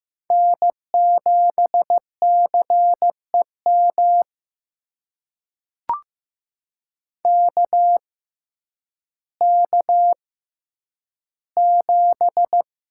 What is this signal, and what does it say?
Morse code, heard as a keyed tone, about 15 wpm, spaced as for 9 wpm, 700 Hz. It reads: N7CEM KK7
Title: USA Call Signs - 5 Characters - 6x Character Spacing - 15wpm